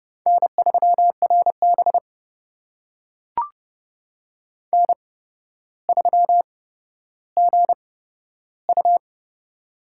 Morse code, heard as a keyed tone, about 30 wpm, spaced as for 13 wpm, 700 Hz. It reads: N3RB N3GU
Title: USA Call Signs - 4 Characters - 8x Character Spacing - 30wpm